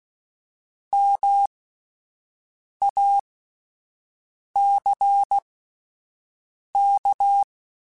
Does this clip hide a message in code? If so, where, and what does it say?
Yes: Morse code from 0.9 to 7.4 s, heard as a keyed tone, about 16 wpm, spaced as for 5 wpm, 800 Hz. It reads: MACK